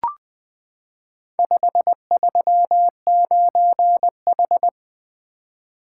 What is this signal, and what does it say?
Morse code, heard as a keyed tone, about 20 wpm, 700 Hz. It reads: 539H
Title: Four Letter-Number Combinations - 20wpm